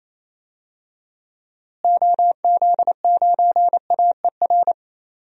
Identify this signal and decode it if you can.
Morse code, heard as a keyed tone, about 28 wpm, 700 Hz. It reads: OZ9AER
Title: DXCC-100 Call Signs - 6 Characters - Repeated 2x - 28wpm